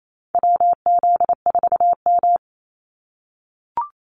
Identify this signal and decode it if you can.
Morse code, heard as a keyed tone, about 28 wpm, 700 Hz. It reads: WZ4M
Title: NA SOTA Activator Callsigns 28wpm